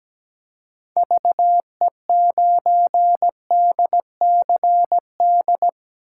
Morse code, heard as a keyed tone, about 17 wpm, 700 Hz. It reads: VE9DCD
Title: K1USN SST QSO Practice - 17wpm